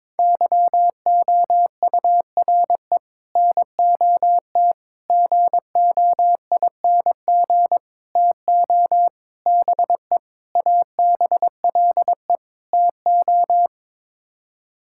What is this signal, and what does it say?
Morse code, heard as a keyed tone, about 22 wpm, 700 Hz. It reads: YOURE NOT GOING TO BE ABLE TO
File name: practice / 22wpm / Sets of 7 Words - 22wpm